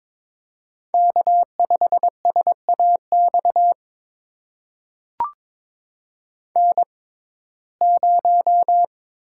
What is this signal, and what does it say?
Morse code, heard as a keyed tone, about 22 wpm, spaced as for 16 wpm, 700 Hz. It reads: K5SAX N0
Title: USA Call Signs - 5 Characters - 6x Character Spacing - 22wpm